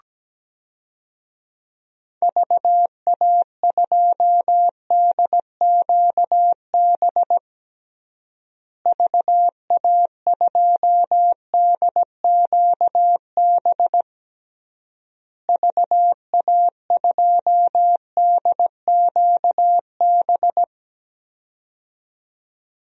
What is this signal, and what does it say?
Morse code, heard as a keyed tone, about 17 wpm, 700 Hz. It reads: VA2DQB VA2DQB VA2DQB
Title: DXCC-100 Call Signs - 6 Characters - Repeated 3x - 17wpm